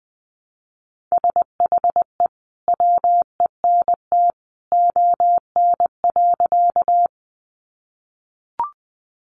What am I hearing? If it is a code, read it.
Morse code, heard as a keyed tone, about 20 wpm, 700 Hz. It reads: SHE WENT ON.
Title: Sentences from Top 200 Words 20wpm